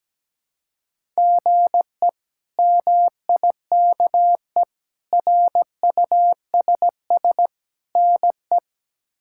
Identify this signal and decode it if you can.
Morse code, heard as a keyed tone, about 17 wpm, 700 Hz. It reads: GE MIKE RUSS NE
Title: K1USN SST QSO Practice - 17wpm